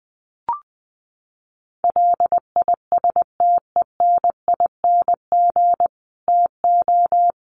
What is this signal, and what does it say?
Morse code, heard as a keyed tone, about 20 wpm, 700 Hz. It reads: LISTENING TO
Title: Sets of 5 Words 20wpm